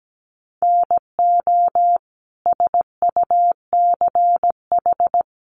Morse code, heard as a keyed tone, about 17 wpm, 700 Hz. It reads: NO SUCH
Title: Sets of 5 Words Encore 17wpm